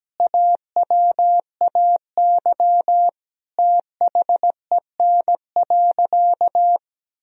Morse code, heard as a keyed tone, about 17 wpm, 700 Hz. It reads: AWAY THEN.
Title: Sentences from Top 1000 Words - Review Part 4 - 17wpm